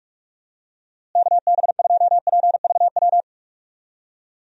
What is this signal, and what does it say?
Morse code, heard as a keyed tone, about 46 wpm, 700 Hz. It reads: KB2PVW
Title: USA Call Signs - 6 Characters - Repeated 2x - 45wpm